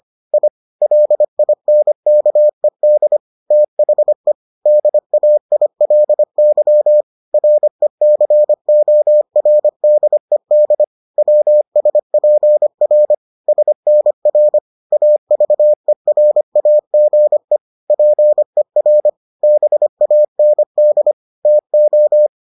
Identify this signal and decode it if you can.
Morse code, heard as a keyed tone, about 25 wpm, 600 Hz. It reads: I LINKED THE DAILY RECORDED WSPR SNR AVERAGE PER BAND TO